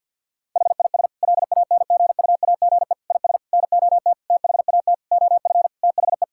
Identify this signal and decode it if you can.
Morse code, heard as a keyed tone, about 50 wpm, 700 Hz. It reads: HIS LANGUAGE IS NOT THAT OF THE
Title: Sentences from Top 1600 Words 50wpm